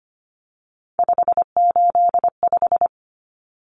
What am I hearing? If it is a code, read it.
Morse code, heard as a keyed tone, about 25 wpm, 700 Hz. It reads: 585
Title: Common Cut Numbers - 3-Digit 25wpm